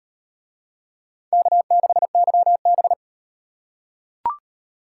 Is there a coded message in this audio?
Yes: Morse code from 1.3 to 2.9 s, heard as a keyed tone, about 38 wpm, 700 Hz. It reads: K6YB